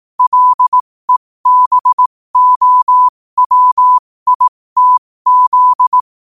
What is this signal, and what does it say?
Morse code, heard as a keyed tone, about 18 wpm, spaced as for 15 wpm, 1000 Hz. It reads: LEBOWITZ
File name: qotdfiles / 15wpm